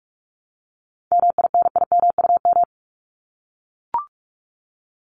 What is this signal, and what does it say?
Morse code, heard as a keyed tone, about 44 wpm, 700 Hz. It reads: MINIMUM
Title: Top 1251-1500 Words 45wpm